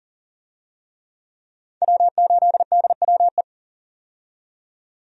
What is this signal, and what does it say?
Morse code, heard as a keyed tone, about 40 wpm, 700 Hz. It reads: W8DWE
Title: USA Call Signs - 5 Characters - 40wpm